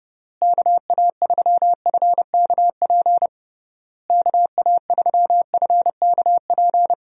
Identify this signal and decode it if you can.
Morse code, heard as a keyed tone, about 30 wpm, 700 Hz. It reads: KA3FKP KA3FKP
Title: USA Call Signs - 6 Characters - Repeated 3x - 30wpm